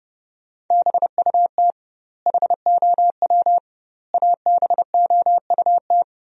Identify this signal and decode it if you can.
Morse code, heard as a keyed tone, about 31 wpm, 700 Hz. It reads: BUT HOW ABOUT
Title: 2x Word-Spacing - Sentences from Top 100 Words - 30wpm